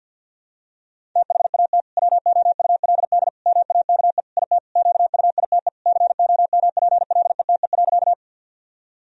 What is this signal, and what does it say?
Morse code, heard as a keyed tone, about 52 wpm, 700 Hz. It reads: THAT WOULD MAKE IT QUITE COMPLETE.
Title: Sentences from Top 1000 Words - Review Part 2 - 50wpm